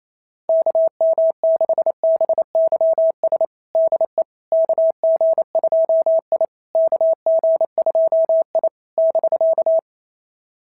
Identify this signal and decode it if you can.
Morse code, heard as a keyed tone, about 28 wpm, 690 Hz. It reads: KM6BYS DE KG2I KG2I <BK>
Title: SOTA QSO Practice 28wpm